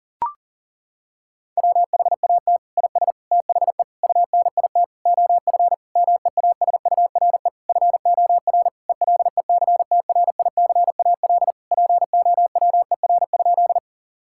Morse code, heard as a keyed tone, about 40 wpm, 700 Hz. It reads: WHAT IS THE UNIT OF MEASURE FOR ELECTRICAL POWER?
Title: Sentences from Top 1000 Words - Review Part 6 - 40wpm